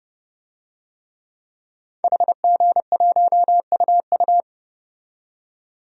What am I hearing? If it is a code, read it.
Morse code, heard as a keyed tone, about 29 wpm, 700 Hz. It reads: HG1UU
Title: DXCC-100 Call Signs - 5 Characters - 30wpm